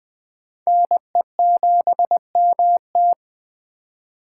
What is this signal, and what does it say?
Morse code, heard as a keyed tone, about 20 wpm, 700 Hz. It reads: NE7MT
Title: Speed Racing - USA 2x2 Call Signs 17wpm